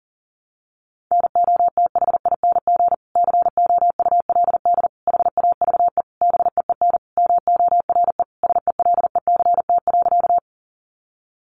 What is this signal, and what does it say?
Morse code, heard as a keyed tone, about 37 wpm, 700 Hz. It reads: NOTHING COULD HAVE BEEN MORE SELECT.
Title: Sentences from Top 1300 Words 40wpm